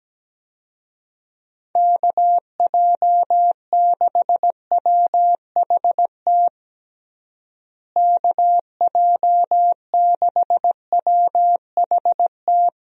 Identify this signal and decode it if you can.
Morse code, heard as a keyed tone, about 17 wpm, 700 Hz. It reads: KJ6WHT KJ6WHT
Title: USA Call Signs - 6 Characters - Repeated 3x - 17wpm